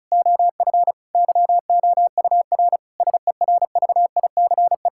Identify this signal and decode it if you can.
Morse code, heard as a keyed tone, about 36 wpm, 690 Hz. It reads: OF YOUR SERVICE
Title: Sentences from Top 1000 Words - Review Part 5 - 35wpm